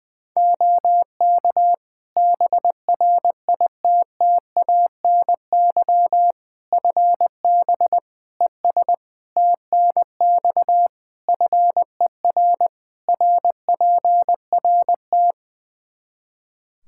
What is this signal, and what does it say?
Morse code, heard as a keyed tone, about 21 wpm, 700 Hz. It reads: OK BRITTANY FB ES TNX FER RPRT
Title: K1USN SST QSO Practice 20wpm